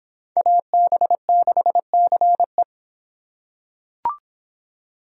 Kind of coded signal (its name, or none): Morse code